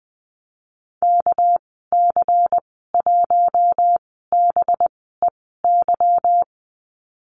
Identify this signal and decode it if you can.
Morse code, heard as a keyed tone, about 20 wpm, spaced as for 14 wpm, 700 Hz. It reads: KC1BEY
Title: 200 Lesson 32g - Letter k - Call Signs 20wpm